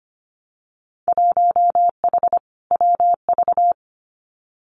Morse code, heard as a keyed tone, about 25 wpm, 700 Hz. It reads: 1H WV
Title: Winter Field Day Exchanges Encore 25wpm